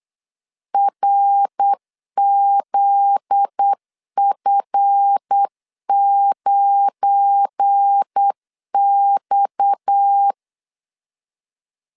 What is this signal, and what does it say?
Morse code, heard as a keyed tone, about 8 wpm, 800 Hz. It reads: RZF9X
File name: LEZIONI / 7 GF9Z / 7 GF9Z 8wpm 3213